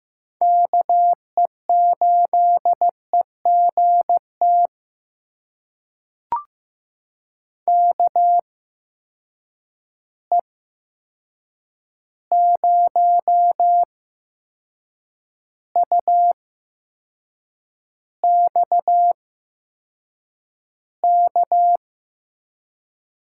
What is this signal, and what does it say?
Morse code, heard as a keyed tone, about 15 wpm, spaced as for 6 wpm, 700 Hz. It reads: KE8EGT KE0UXK
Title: USA Call Signs - 6 Characters - 8x Character Spacing - 15wpm